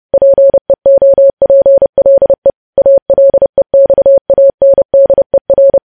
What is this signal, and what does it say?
Morse code, heard as a keyed tone, about 30 wpm, 550 Hz. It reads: PEOPLE ALEXANDER
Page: news headlines in morse code at 30 wpm